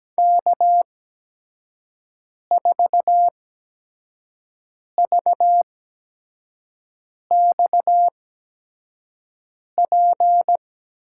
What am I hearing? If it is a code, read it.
Morse code, heard as a keyed tone, about 17 wpm, spaced as for 5 wpm, 700 Hz. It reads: K4VXP
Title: USA Call Signs - 5 Characters - 8x Character Spacing - 17wpm